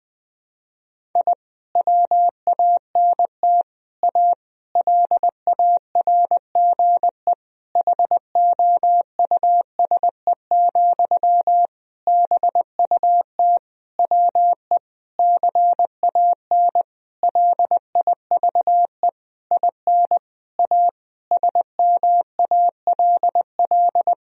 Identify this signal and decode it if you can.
Morse code, heard as a keyed tone, about 20 wpm, 700 Hz. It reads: I WANT A LARGE HOUSE, BUT WE CAN LIVE IN A SMALL
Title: Sentences from Top 1000 Words - Review Part 1 - 20wpm